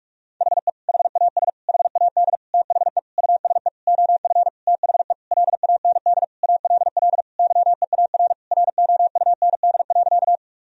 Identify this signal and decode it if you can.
Morse code, heard as a keyed tone, about 41 wpm, 700 Hz. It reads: HE HAS HAD THE USE OF THE LAND ALL YEAR ROUND.
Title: Sentences from Top 1000 Words - Review Part 1 - 45wpm